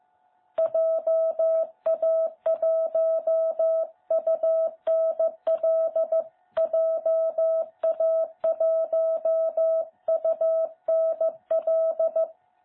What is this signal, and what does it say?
Morse code, heard as a keyed tone, about 14 wpm, 660 Hz. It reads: JA1UNL JA1UNL
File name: jh8cbh / 2021cw